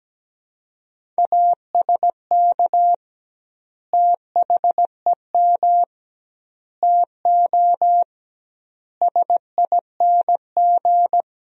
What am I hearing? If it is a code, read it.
Morse code, heard as a keyed tone, about 17 wpm, 700 Hz. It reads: ASK THEM TO SING
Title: 2x Word-Spacing - Sentences from Top 400 Words - 17wpm